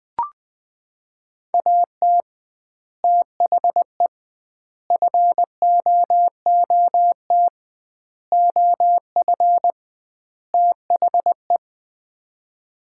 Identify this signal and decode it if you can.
Morse code, heard as a keyed tone, about 20 wpm, 700 Hz. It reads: AT THE FOOT OF THE
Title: 2x Word-Spacing - Sets of 5 Words - 20wpm